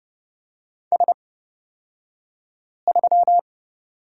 Morse code, heard as a keyed tone, about 30 wpm, spaced as for 5 wpm, 700 Hz. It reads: S3
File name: Lesson 38d - Letter q - Two Characters 30wpm